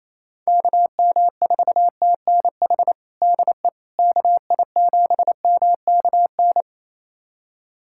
Morse code, heard as a keyed tone, about 28 wpm, 700 Hz. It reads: KM4TNH DE KI7MKN
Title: USA Callsign DE Callsign - 6 Characters - 28wpm